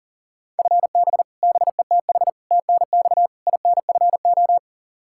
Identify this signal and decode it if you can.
Morse code, heard as a keyed tone, about 40 wpm, 700 Hz. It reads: FB BETH TNX INFO
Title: Practice QSO - Exchange 3 - More Info - 40wpm